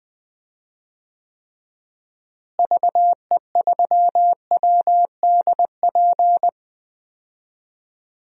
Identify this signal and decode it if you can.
Morse code, heard as a keyed tone, about 20 wpm, 700 Hz. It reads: VE3WDP
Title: DXCC-100 Call Signs - 6 Characters - Repeated 3x - 20wpm